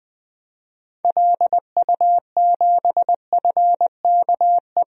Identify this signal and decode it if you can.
Morse code, heard as a keyed tone, about 20 wpm, 700 Hz. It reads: LU7FKE
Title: DXCC-100 Call Signs - 6 Characters - 20wpm